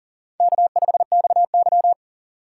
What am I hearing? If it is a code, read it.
Morse code, heard as a keyed tone, about 39 wpm, 690 Hz. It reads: K5XY
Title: USA Call Signs - 4 Characters - 2x Character Spacing - 40wpm